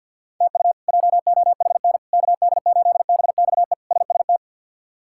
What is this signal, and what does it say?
Morse code, heard as a keyed tone, about 52 wpm, 700 Hz. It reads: TU JOHN KD8BXE SST
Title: K1USN SST QSO Practice - 50wpm